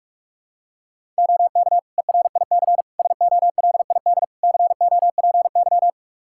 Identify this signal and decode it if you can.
Morse code, heard as a keyed tone, about 46 wpm, 700 Hz. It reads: OK ERIC SOLID COPY
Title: Practice QSO - Exchange 3 - More Info - 45wpm